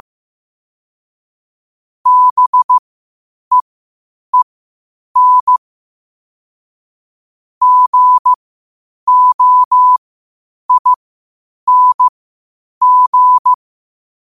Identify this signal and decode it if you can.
Morse code, heard as a keyed tone, about 15 wpm, spaced as for 8 wpm, 1000 Hz. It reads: BEEN GOING
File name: qotdfiles / 8wpm